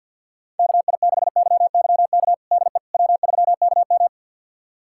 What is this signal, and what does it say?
Morse code, heard as a keyed tone, about 51 wpm, 700 Hz. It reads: KI6YYK DE W3KM
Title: USA Callsign DE Callsign 50wpm